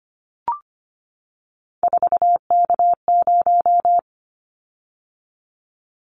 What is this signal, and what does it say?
Morse code, heard as a keyed tone, about 25 wpm, 700 Hz. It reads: 4K0
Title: Callsign Prefixes for Top 100 DXCC - 25wpm